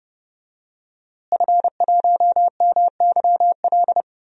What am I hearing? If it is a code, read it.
Morse code, heard as a keyed tone, about 30 wpm, 700 Hz. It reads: F1MYL
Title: DXCC-100 Call Signs - 5 Characters - Repeated 3x - 30wpm